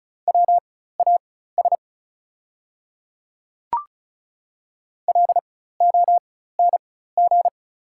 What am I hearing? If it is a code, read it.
Morse code, heard as a keyed tone, about 36 wpm, spaced as for 16 wpm, 700 Hz. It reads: WAS LONG
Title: 4x Character Spacing - Top 100 Words - 35wpm